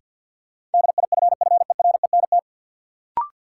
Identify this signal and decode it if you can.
Morse code, heard as a keyed tone, about 48 wpm, 700 Hz. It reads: DIFFERENT
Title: Sentences from Top 600 Words - New Word Drill - 50wpm